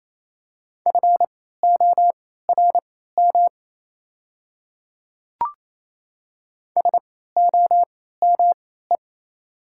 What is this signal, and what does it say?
Morse code, heard as a keyed tone, about 27 wpm, spaced as for 16 wpm, 690 Hz. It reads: FORM SOME
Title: 3x Character Spacing - Top 300 Words - 28wpm